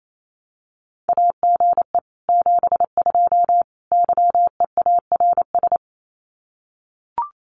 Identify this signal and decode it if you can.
Morse code, heard as a keyed tone, about 28 wpm, 700 Hz. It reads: AGE 72 YEARS
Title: Common QSO Phrases Encore 28wpm